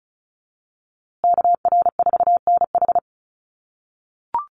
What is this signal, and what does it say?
Morse code, heard as a keyed tone, about 34 wpm, 700 Hz. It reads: KR4NH
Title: USA Call Signs Repeated 3x 35wpm